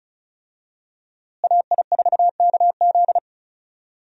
Morse code, heard as a keyed tone, about 36 wpm, 700 Hz. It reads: AI4KZ
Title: USA Call Signs - 5 Characters - Repeated 3x - 35wpm